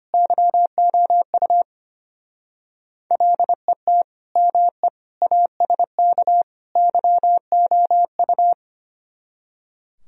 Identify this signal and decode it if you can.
Morse code, heard as a keyed tone, about 31 wpm, 700 Hz. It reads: YOU LET ME ASK YOU
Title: Speed-Racing - Sets of 4 Words 25wpm